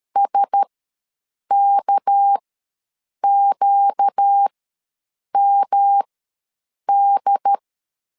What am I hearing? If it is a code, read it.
Morse code, heard as a keyed tone, about 13 wpm, spaced as for 7 wpm, 800 Hz. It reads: SKQMD
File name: LEZIONI / 9 286 ws 12wpm 3351